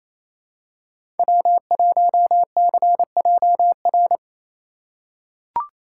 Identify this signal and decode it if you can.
Morse code, heard as a keyed tone, about 28 wpm, 700 Hz. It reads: W1CJR
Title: USA Call Signs - 5 Characters - 2x Character Spacing - 28wpm